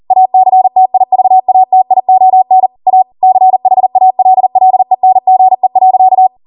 Morse code, heard as a keyed tone, about 40 wpm, 750 Hz. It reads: ACTIVATION A CHALLENGE.